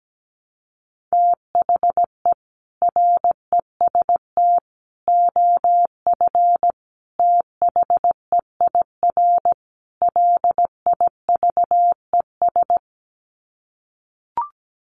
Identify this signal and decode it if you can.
Morse code, heard as a keyed tone, about 17 wpm, 700 Hz. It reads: THE REST OF THEIR LIVES
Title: Sets of 5 Words Encore 17wpm